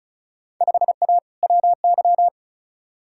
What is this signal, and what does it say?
Morse code, heard as a keyed tone, about 36 wpm, 700 Hz. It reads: 5A WY
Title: Field Day Exchanges Encore - 35wpm